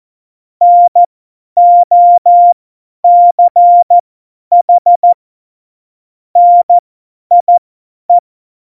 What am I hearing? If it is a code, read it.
Morse code, heard as a keyed tone, about 14 wpm, spaced as for 10 wpm, 700 Hz. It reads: NOCH NIE
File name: Morsekurs / Witze / Witz 12 Mai 2020, 50 F70